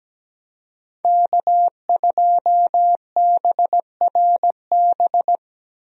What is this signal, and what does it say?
Morse code, heard as a keyed tone, about 17 wpm, 700 Hz. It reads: K2BRB
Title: USA Call Signs - 5 Characters - Repeated 2x - 17wpm